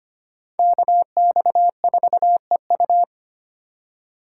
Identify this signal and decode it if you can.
Morse code, heard as a keyed tone, about 32 wpm, 700 Hz. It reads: KX4EU